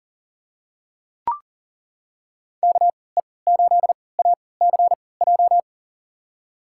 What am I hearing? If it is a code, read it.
Morse code, heard as a keyed tone, about 37 wpm, spaced as for 22 wpm, 700 Hz. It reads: KE8ACJ
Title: USA Call Signs - 3x Character Spacing - 40wpm